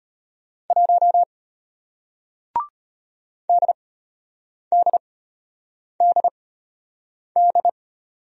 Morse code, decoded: 1 DDDD

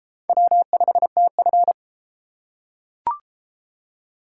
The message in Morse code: W5TF

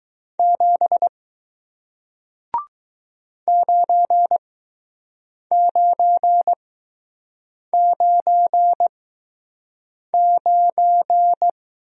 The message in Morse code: 7 9999

